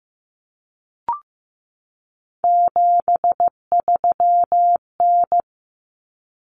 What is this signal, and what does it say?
Morse code, heard as a keyed tone, about 15 wpm, 700 Hz. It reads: 73N
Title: Common Cut Numbers - 3-Digit 15wpm